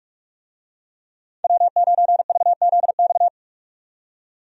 Morse code, heard as a keyed tone, about 47 wpm, 700 Hz. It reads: W9VZX
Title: USA Call Signs - 5 Characters - 45wpm